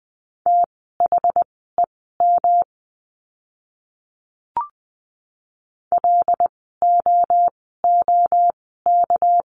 Morse code, heard as a keyed tone, about 20 wpm, spaced as for 15 wpm, 700 Hz. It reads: THEM LOOK
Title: 237 Lesson 38f - Letter q - Words 20wpm